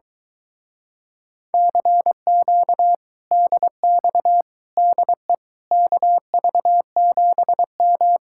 Morse code, heard as a keyed tone, about 22 wpm, 700 Hz. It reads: CQ DX DE KV7M